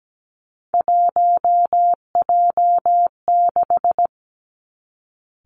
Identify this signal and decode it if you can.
Morse code, heard as a keyed tone, about 17 wpm, 700 Hz. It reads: 1J6
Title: Three Challenging Letter-Number Combinations 17wpm